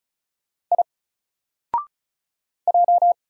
Morse code, heard as a keyed tone, about 33 wpm, 700 Hz. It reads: I J
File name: Single-Letter-Number-35wpm